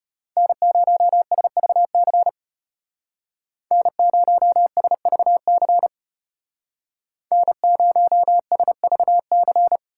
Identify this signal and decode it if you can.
Morse code, heard as a keyed tone, about 33 wpm, 700 Hz. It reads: N0SVC N0SVC N0SVC